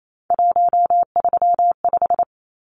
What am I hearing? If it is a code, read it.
Morse code, heard as a keyed tone, about 28 wpm, 700 Hz. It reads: 135
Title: Numbers - 3-Digit 28wpm